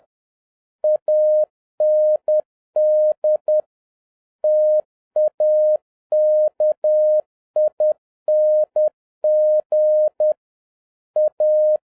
Morse code, heard as a keyed tone, about 10 wpm, 610 Hz. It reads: AND TAKING A
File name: 2011-07-mp3 / n46 JOHN K3TN 10wpm